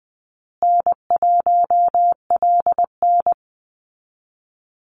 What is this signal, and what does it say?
Morse code, heard as a keyed tone, about 20 wpm, 700 Hz. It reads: N1LN
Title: Easy Call Signs Repeated 3x - 20wpm